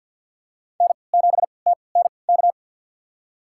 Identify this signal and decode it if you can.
Morse code, heard as a keyed tone, about 51 wpm, spaced as for 28 wpm, 700 Hz. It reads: N7TNK
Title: USA Call Signs - 5 Characters - 3x Character Spacing - 50wpm